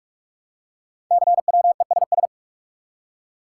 Morse code, heard as a keyed tone, about 44 wpm, 700 Hz. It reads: CWESS
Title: Code Groups - 5 Letters - 45wpm